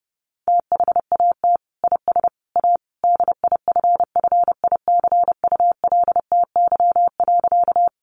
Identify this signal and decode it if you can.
Morse code, heard as a keyed tone, about 28 wpm, 700 Hz. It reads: THAT IS A DIFFICULTY.
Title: Sentences from Top 1500 Words 30wpm